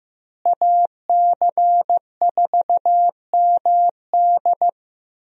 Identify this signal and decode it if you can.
Morse code, heard as a keyed tone, about 15 wpm, 700 Hz. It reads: AC4MD
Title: USA Call Signs - 5 Characters - Repeated 3x - 15wpm